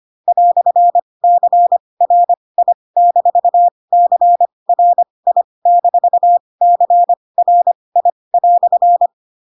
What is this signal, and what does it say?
Morse code, heard as a keyed tone, about 25 wpm, spaced as for 20 wpm, 700 Hz. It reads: "CRI-CRI-CRI"
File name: Pinocchio / Pinocchio ch4 15 25 ik1hge